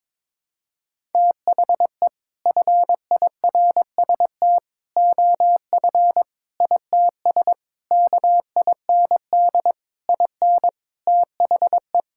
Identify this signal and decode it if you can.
Morse code, heard as a keyed tone, about 22 wpm, 700 Hz. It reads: THE FIRST OF ITS KIND IN THE
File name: practice / 22wpm / Sets of 7 Words - 22wpm